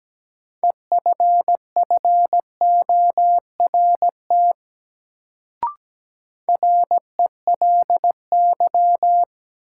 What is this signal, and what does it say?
Morse code, heard as a keyed tone, about 17 wpm, 700 Hz. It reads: EFFORT RELY
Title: Sentences from Top 1200 Words - New Word Drill Alt - 17wpm